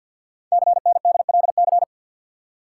Morse code, heard as a keyed tone, about 50 wpm, 690 Hz. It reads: CNDLC
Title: Code Groups - 5 Letters - 50wpm